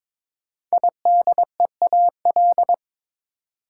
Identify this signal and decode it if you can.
Morse code, heard as a keyed tone, about 22 wpm, 700 Hz. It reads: IDEAL
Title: Sentences from Top 1400 Words - New Word Drill - 22wpm